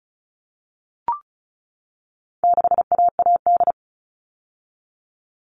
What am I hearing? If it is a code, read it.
Morse code, heard as a keyed tone, about 34 wpm, 700 Hz. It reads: 6AAD
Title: All Cut Numbers - 4-Digit 35wpm